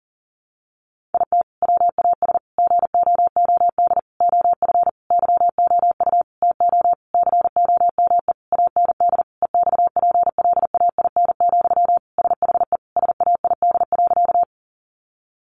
Morse code, heard as a keyed tone, about 37 wpm, 700 Hz. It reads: IT WAS GOOD OF YOU TO COME AND EXPLAIN, SHE SAID.
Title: Sentences from Top 1100 Words 40wpm